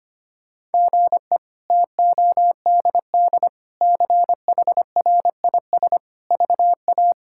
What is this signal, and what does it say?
Morse code, heard as a keyed tone, about 25 wpm, 700 Hz. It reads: GE TODD CHRIS VA
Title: K1USN SST QSO Practice - 25wpm